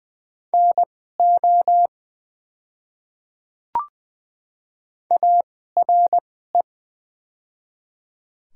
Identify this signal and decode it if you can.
Morse code, heard as a keyed tone, about 20 wpm, spaced as for 14 wpm, 700 Hz. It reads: NO ARE
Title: 2x Character Spacing - Top 300 Words - 20wpm